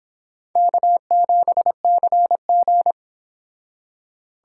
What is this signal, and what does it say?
Morse code, heard as a keyed tone, about 25 wpm, 700 Hz. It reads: K7CG